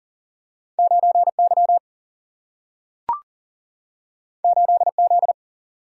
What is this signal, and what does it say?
Morse code, heard as a keyed tone, about 40 wpm, 700 Hz. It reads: 9Y 8Z
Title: Single-Letter Single-Number 40wpm